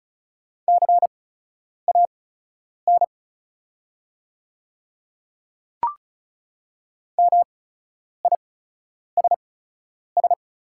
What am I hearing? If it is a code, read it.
Morse code, heard as a keyed tone, about 36 wpm, spaced as for 10 wpm, 700 Hz. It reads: CAN MISS
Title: 8x Character Spacing - Top 500 Words - 35wpm